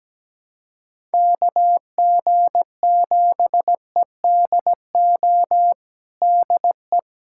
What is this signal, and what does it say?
Morse code, heard as a keyed tone, about 17 wpm, 700 Hz. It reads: KG7EDO DE
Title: USA Callsign DE Callsign 17wpm